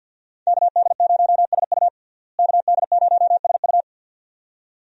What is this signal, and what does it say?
Morse code, heard as a keyed tone, about 46 wpm, 700 Hz. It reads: KD0SU KD0SU
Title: USA Call Signs - 5 Characters - Repeated 2x - 50wpm